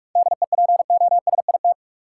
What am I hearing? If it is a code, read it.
Morse code, heard as a keyed tone, about 45 wpm, 700 Hz. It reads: DEPOSIT
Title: Sentences from Top 1300 Words - New Word Drill - 45wpm